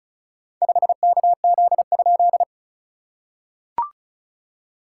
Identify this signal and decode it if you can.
Morse code, heard as a keyed tone, about 34 wpm, 690 Hz. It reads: 5KZ?
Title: SOTA QSO Practice 35wpm